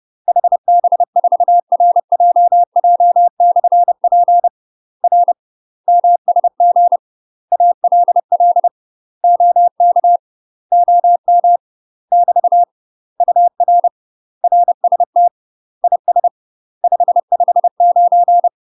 Morse code, heard as a keyed tone, about 30 wpm, 700 Hz. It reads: HB4RJJ/P R MSG ALL OK OM <BT> UR RST IS 559
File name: UK MASTER QSOs TRAINING / 30WPM / UK MASTER QSOs 0033